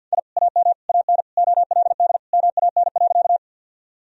Morse code, heard as a keyed tone, about 50 wpm, 700 Hz. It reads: I AM AN OLD MAN.
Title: Sentences from Top 1000 Words - Review Part 2 - 50wpm